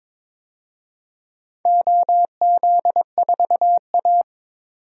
Morse code, heard as a keyed tone, about 22 wpm, 700 Hz. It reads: OZ4A
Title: DXCC-100 Call Signs - 4 Characters - 22wpm